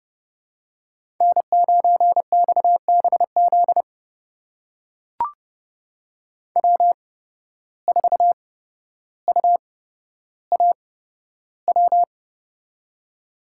Morse code, heard as a keyed tone, about 30 wpm, spaced as for 13 wpm, 700 Hz. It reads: N9XBZ W4UAW
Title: USA Call Signs - 5 Characters - 8x Character Spacing - 30wpm